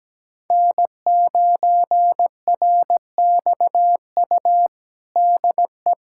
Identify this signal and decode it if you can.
Morse code, heard as a keyed tone, about 17 wpm, 700 Hz. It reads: N9RXU DE
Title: USA Callsign DE Callsign - 5 Characters - 17wpm